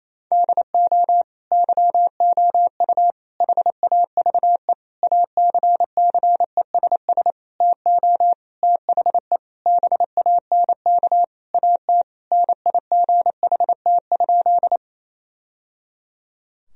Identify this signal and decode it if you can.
Morse code, heard as a keyed tone, about 28 wpm, 700 Hz. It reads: DO YOU HAVE ACCESS TO THE BANK AT NIGHT?
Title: Sentences from Top 1000 Words - Review Part 4 - 28wpm